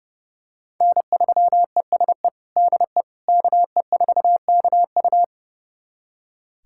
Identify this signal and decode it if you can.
Morse code, heard as a keyed tone, about 29 wpm, 700 Hz. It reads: N3ESE DE KE4KU
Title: USA Callsign DE Callsign - 5 Characters - 30wpm